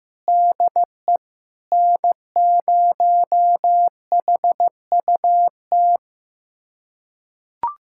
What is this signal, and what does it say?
Morse code, heard as a keyed tone, about 15 wpm, 700 Hz. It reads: DE N0HUT
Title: USA Callsign DE Callsign - 5 Characters - 15wpm